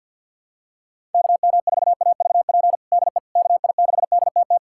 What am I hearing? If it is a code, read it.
Morse code, heard as a keyed tone, about 46 wpm, 700 Hz. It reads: KM4AVP DE KI6DTT
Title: USA Callsign DE Callsign - 6 Characters - 50wpm